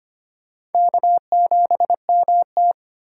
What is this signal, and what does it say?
Morse code, heard as a keyed tone, about 32 wpm, 700 Hz. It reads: K7MT